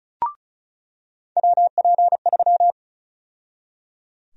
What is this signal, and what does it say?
Morse code, heard as a keyed tone, about 36 wpm, 700 Hz. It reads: WP3
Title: USA Call Sign Prefixes - 35wpm